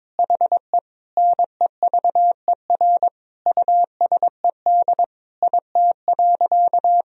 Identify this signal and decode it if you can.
Morse code, heard as a keyed tone, about 22 wpm, 700 Hz. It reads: HE NEVER USED IT.
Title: Sentences from Top 1000 Words - Review Part 2 - 22wpm